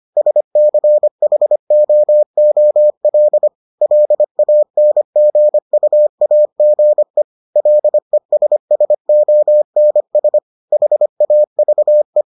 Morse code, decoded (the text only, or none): SCHOOL LANGUAGE LESSONS HAVE